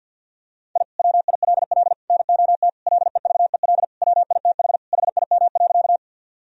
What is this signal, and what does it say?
Morse code, heard as a keyed tone, about 50 wpm, 700 Hz. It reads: I WILL NOT LEVEL WITH HIM.
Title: Sentences from Top 1000 Words - Review Part 4 - 50wpm